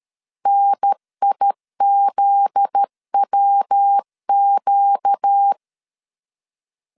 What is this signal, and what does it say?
Morse code, heard as a keyed tone, about 13 wpm, 800 Hz. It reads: NIZWQ
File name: LEZIONI / 8 YJ45 / 8 Y 12wpm 2228